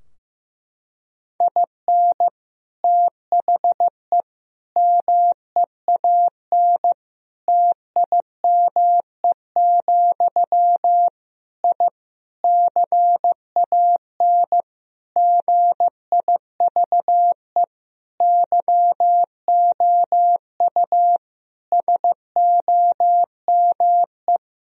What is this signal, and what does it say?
Morse code, heard as a keyed tone, about 15 wpm, 700 Hz. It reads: IN THE MEAN TIME, I CAN GIVE YOU SOME
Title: Sentences from Top 1000 Words - Review Part 3 - 15wpm